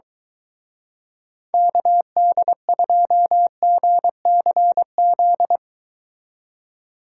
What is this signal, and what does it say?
Morse code, heard as a keyed tone, about 22 wpm, 700 Hz. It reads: KD2GCZ